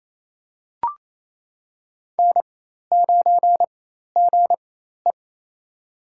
Morse code, heard as a keyed tone, about 27 wpm, spaced as for 13 wpm, 700 Hz. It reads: N9GE
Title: USA Call Signs - 4 Characters - 4x Character Spacing - 28wpm